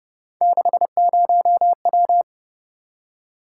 Morse code, heard as a keyed tone, about 28 wpm, 700 Hz. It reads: 60W